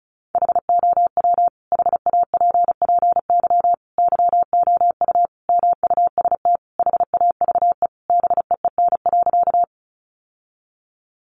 Morse code, heard as a keyed tone, about 34 wpm, 700 Hz. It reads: HOW HAPPY YOU MUST HAVE BEEN.
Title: Sentences from Top 1000 Words 35wpm